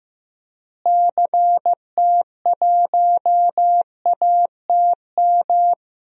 Morse code, heard as a keyed tone, about 15 wpm, 700 Hz. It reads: CT1ATM
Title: DXCC-100 Call Signs - 6 Characters - Repeated 2x - 15wpm